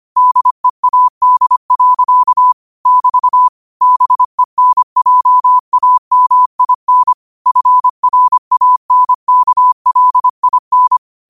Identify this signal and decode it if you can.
Morse code, heard as a keyed tone, about 25 wpm, 1000 Hz. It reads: DEAD. <BT> BENJAMIN FRANKLIN
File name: 25wpm